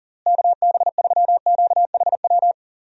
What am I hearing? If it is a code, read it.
Morse code, heard as a keyed tone, about 40 wpm, 700 Hz. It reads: KB3QHW
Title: USA Call Signs - 6 Characters - Repeated 2x - 40wpm